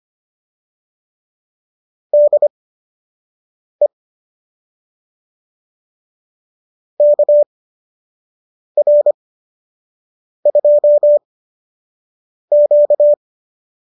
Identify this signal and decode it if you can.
Morse code, heard as a keyed tone, about 25 wpm, spaced as for 6 wpm, 590 Hz. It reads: DE KR2Q